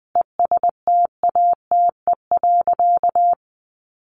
Morse code, heard as a keyed tone, about 20 wpm, 700 Hz. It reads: ESTATE.
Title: Sentences from Top 1200 Words 20wpm